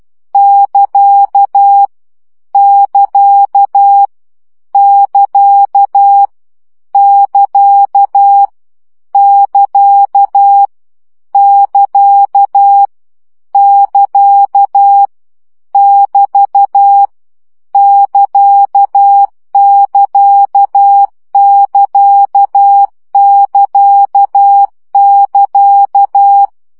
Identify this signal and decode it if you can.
Morse code, heard as a keyed tone, about 12 wpm, 800 Hz. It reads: <KA> <KA> <KA> <KA> <KA> <KA> <KA> <BT> <KA><KA><KA><KA><KA>
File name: CW gyakorlas ct 10 baud